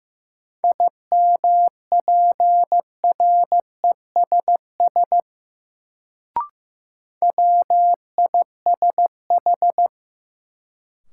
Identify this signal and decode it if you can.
Morse code, heard as a keyed tone, about 15 wpm, 700 Hz. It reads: IMPRESS WISH
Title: Sentences from Top 1600 Words - New Word Drill - 15wpm